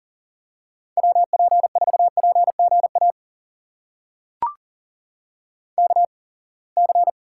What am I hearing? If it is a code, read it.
Morse code, heard as a keyed tone, about 40 wpm, spaced as for 28 wpm, 700 Hz. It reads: WP4PGA KC
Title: USA Call Signs - 8x Character Spacing - 40wpm